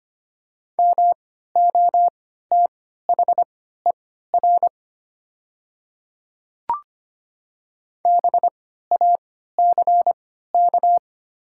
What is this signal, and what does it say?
Morse code, heard as a keyed tone, about 25 wpm, spaced as for 14 wpm, 700 Hz. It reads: MOTHER BACK
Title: 3x Character Spacing - Top 500 Words - 25wpm